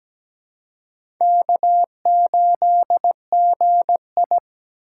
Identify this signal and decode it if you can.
Morse code, heard as a keyed tone, about 17 wpm, 700 Hz. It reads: K8GI
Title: USA Call Signs - 4 Characters - 3x Character Spacing - 17wpm